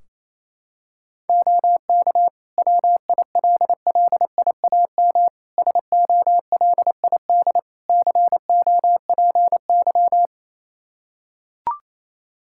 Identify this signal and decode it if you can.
Morse code, heard as a keyed tone, about 28 wpm, 700 Hz. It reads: OK WILLIAM SOLID COPY
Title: Practice QSO - Exchange 3 - More Info - 28wpm